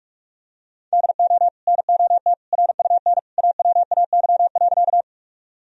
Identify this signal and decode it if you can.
Morse code, heard as a keyed tone, about 41 wpm, 700 Hz. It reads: DO NOT RUN AWAY.
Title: Sentences from Top 1000 Words - Review Part 1 - 45wpm